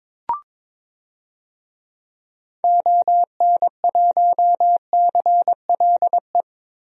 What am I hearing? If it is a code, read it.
Morse code, heard as a keyed tone, about 22 wpm, 700 Hz. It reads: ON1CLE
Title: DXCC-100 Call Signs - 6 Characters - Repeated 2x - 22wpm